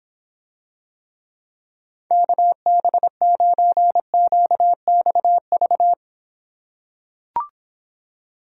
Morse code, heard as a keyed tone, about 26 wpm, 700 Hz. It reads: KB9QXV